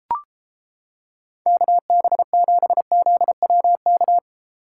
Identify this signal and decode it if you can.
Morse code, heard as a keyed tone, about 31 wpm, 700 Hz. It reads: KB7ZWK